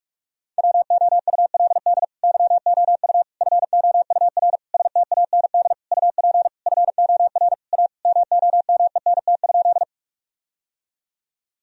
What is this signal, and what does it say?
Morse code, heard as a keyed tone, about 41 wpm, 700 Hz. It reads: WOULD YOU FOUR STAND UP FOR A MOMENT?
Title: Sentences from Top 1000 Words - Review Part 1 - 45wpm